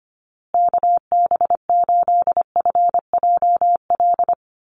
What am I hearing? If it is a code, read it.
Morse code, heard as a keyed tone, about 25 wpm, 700 Hz. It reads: KB8FJL
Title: USA Call Signs Repeated 3x 25wpm